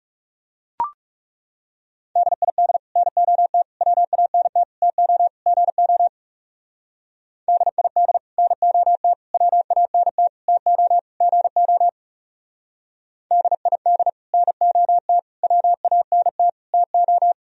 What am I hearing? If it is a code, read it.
Morse code, heard as a keyed tone, about 37 wpm, 700 Hz. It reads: DID NOT WANT TO GO DID NOT WANT TO GO DID NOT WANT TO
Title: Speed-Racing - Sets of 5 Words 30wpm